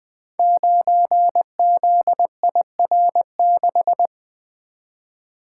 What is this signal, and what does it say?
Morse code, heard as a keyed tone, about 20 wpm, 700 Hz. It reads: 9ZIR6
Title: Code Groups - 5 Letter:Numbers - 20wpm